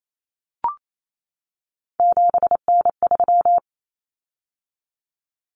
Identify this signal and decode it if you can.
Morse code, heard as a keyed tone, about 28 wpm, 700 Hz. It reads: 7N3
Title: Common Cut Numbers - 3-Digit 28wpm